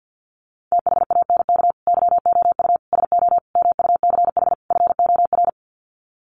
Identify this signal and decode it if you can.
Morse code, heard as a keyed tone, about 50 wpm, 700 Hz. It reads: THANK YOU SO MUCH FOR
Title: Sets of 5 Words 50wpm